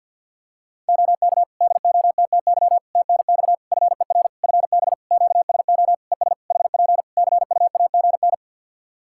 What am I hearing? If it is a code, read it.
Morse code, heard as a keyed tone, about 47 wpm, 700 Hz. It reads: OK DOTTY TNX FER FB QSO ES HP CUAGN
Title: K1USN SST QSO Practice 50wpm